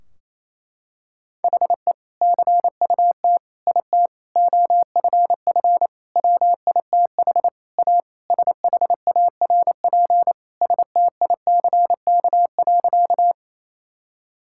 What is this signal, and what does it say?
Morse code, heard as a keyed tone, about 28 wpm, 700 Hz. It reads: HE CUT IT OFF WITH A SHARP STICK.
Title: Sentences from Top 1600 Words 28wpm